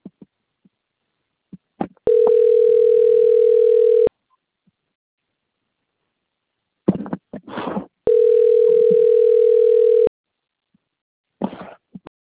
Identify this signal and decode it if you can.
Ringback tone (North America), 440 + 480 Hz.
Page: Sao Roque